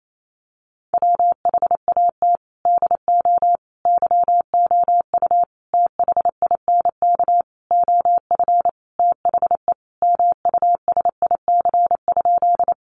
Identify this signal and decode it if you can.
Morse code, heard as a keyed tone, about 28 wpm, 700 Hz. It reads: WHAT DO YOU THINK OF THE MUSIC?
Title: Sentences from Top 300 Words 28wpm